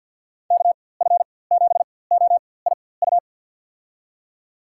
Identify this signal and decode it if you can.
Morse code, heard as a keyed tone, about 51 wpm, spaced as for 23 wpm, 700 Hz. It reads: KF7OIU
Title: USA Call Signs - 6 Characters - 4x Character Spacing - 50wpm